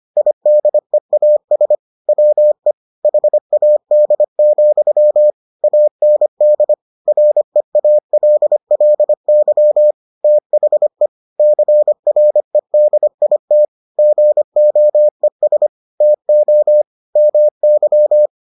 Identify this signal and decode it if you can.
Morse code, heard as a keyed tone, about 25 wpm, 600 Hz. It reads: IDEAS WE HAD, AND REALLY THE CREDIT GOES TO MY